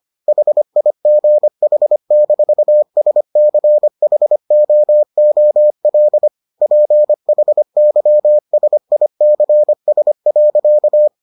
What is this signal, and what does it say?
Morse code, heard as a keyed tone, about 25 wpm, 600 Hz. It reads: HIGH-SCHOOL PHYSICS.